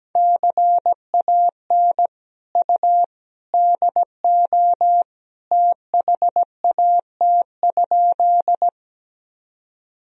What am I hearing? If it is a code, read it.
Morse code, heard as a keyed tone, about 17 wpm, 700 Hz. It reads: CAN U DO THAT?
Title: Sentences from Top 100 Words and Abbreviations 17wpm